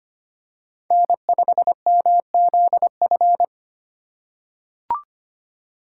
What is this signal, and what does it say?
Morse code, heard as a keyed tone, about 25 wpm, 700 Hz. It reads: N5MZF
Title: USA Call Signs - 5 Characters - 2x Character Spacing - 25wpm